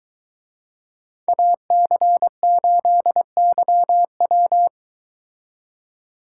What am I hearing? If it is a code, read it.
Morse code, heard as a keyed tone, about 22 wpm, 700 Hz. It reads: AC8YW